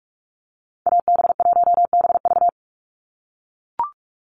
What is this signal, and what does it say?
Morse code, heard as a keyed tone, about 44 wpm, 690 Hz. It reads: AB1BV
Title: USA Call Signs 45wpm